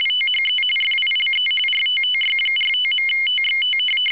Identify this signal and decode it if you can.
RTTY, about 45.45 baud. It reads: THE FILE RTTY45.TXT
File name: w ry850 / rtty45